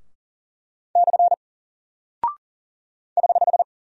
Morse code, heard as a keyed tone, about 41 wpm, 700 Hz. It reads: / <HH>
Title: Single Character (Letter:Number:Punctuation:Prosign) Alt 40wpm